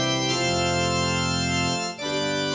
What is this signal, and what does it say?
There is no code.